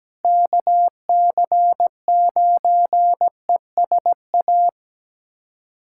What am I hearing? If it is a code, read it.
Morse code, heard as a keyed tone, about 17 wpm, 700 Hz. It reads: KC9ESA
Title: USA Callsign DE Callsign - 6 Characters - 17wpm